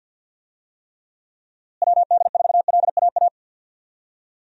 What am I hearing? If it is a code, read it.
Morse code, heard as a keyed tone, about 52 wpm, 700 Hz. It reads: WD4LAA
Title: USA Call Signs - 6 Characters - Repeated 3x - 50wpm